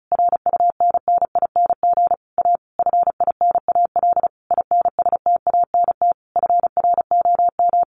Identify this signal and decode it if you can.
Morse code, heard as a keyed tone, about 34 wpm, 700 Hz. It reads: RUNNING A FINAL INSTANT FROM
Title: A Sound Of Thunder 35wpm